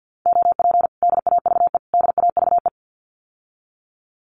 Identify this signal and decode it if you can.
Morse code, heard as a keyed tone, about 50 wpm, 700 Hz. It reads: OP DAVE DAVE
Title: Common QSO Phrases Encore 50wpm